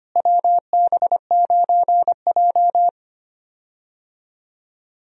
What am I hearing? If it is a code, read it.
Morse code, heard as a keyed tone, about 25 wpm, 700 Hz. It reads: WB9J
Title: USA Call Signs - 4 Characters - Repeated 3x - 25wpm